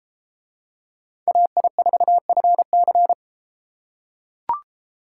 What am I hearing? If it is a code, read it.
Morse code, heard as a keyed tone, about 33 wpm, 700 Hz. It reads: AI4FC